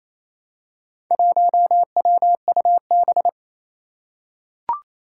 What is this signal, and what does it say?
Morse code, heard as a keyed tone, about 27 wpm, 700 Hz. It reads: 1WUB
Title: Four Challenging Letter-Number Combinations 28wpm